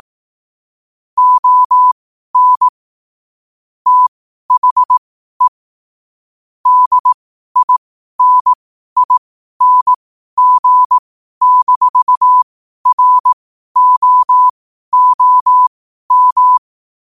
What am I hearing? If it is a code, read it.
Morse code, heard as a keyed tone, about 18 wpm, spaced as for 12 wpm, 1000 Hz. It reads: ON THE DINING-ROOM
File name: qotdfiles / 12wpm